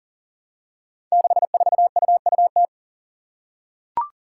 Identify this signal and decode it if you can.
Morse code, heard as a keyed tone, about 39 wpm, 700 Hz. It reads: 64UUT
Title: Five Challenging Letter-Number Combinations 40wpm